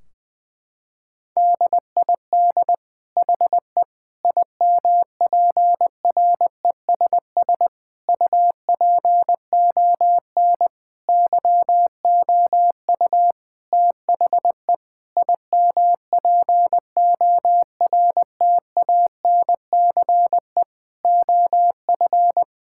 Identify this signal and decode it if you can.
Morse code, heard as a keyed tone, about 20 wpm, 700 Hz. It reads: DID HE IMPRESS UPON YOU THE IMPORTANCE OF
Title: Sentences from Top 1600 Words 20wpm